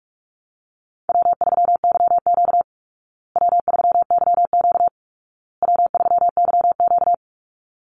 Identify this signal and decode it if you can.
Morse code, heard as a keyed tone, about 46 wpm, 700 Hz. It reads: W3YQ W3YQ W3YQ
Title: Easy Call Signs Repeated 3x - 45wpm